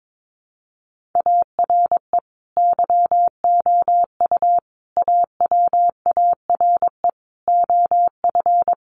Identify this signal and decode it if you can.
Morse code, heard as a keyed tone, about 22 wpm, 700 Hz. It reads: ARE YOU AWARE OF
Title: Sentences from Top 1500 Words 22wpm